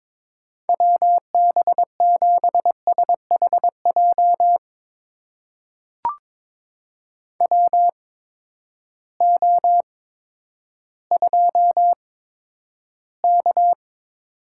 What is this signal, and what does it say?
Morse code, heard as a keyed tone, about 22 wpm, spaced as for 11 wpm, 700 Hz. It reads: WB7SHJ WO2K
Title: USA Call Signs - 8x Character Spacing - 22wpm